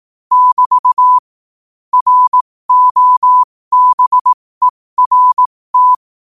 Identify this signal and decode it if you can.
Morse code, heard as a keyed tone, about 18 wpm, spaced as for 15 wpm, 1000 Hz. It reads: <BT> ROBERT
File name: qotdfiles / 15wpm